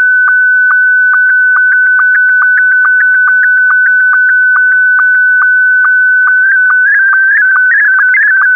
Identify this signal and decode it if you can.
SSTV (slow-scan television), Scottie 1.